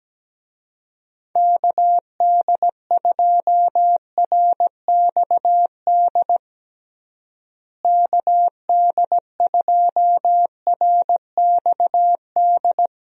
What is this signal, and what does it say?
Morse code, heard as a keyed tone, about 17 wpm, 700 Hz. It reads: KD2RXD KD2RXD
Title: USA Call Signs - 6 Characters - Repeated 2x - 17wpm